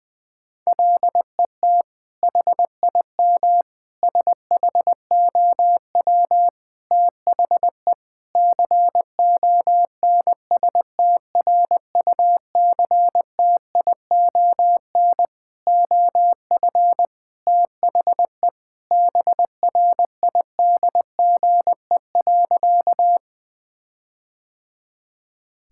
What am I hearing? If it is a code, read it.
Morse code, heard as a keyed tone, about 20 wpm, 700 Hz. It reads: LET HIM SHOW THE CONSTRUCTION OF THE BRIDGE.
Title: Sentences from Top 1600 Words 20wpm